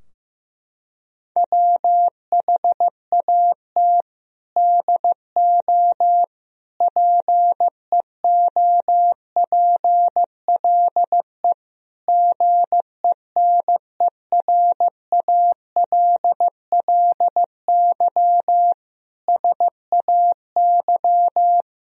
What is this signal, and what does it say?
Morse code, heard as a keyed tone, about 15 wpm, 700 Hz. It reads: WHAT DO PEOPLE GENERALLY SAY
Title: Sentences from Top 1000 Words - Review Part 4 - 15wpm